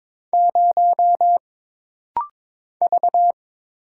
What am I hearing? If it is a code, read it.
Morse code, heard as a keyed tone, about 22 wpm, 700 Hz. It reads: 0 V
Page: This rapid-fire timing is geared to push your ICR (Instant Character Recognition) skills.